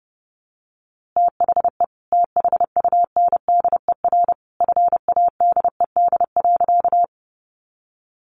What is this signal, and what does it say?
Morse code, heard as a keyed tone, about 30 wpm, 700 Hz. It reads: THE THUNDER FADED.
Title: A Sound Of Thunder 30wpm